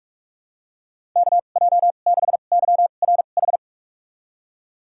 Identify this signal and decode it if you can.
Morse code, heard as a keyed tone, about 46 wpm, spaced as for 32 wpm, 700 Hz. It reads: KJ6YRH
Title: USA Call Signs - 6 Characters - 2x Character Spacing - 45wpm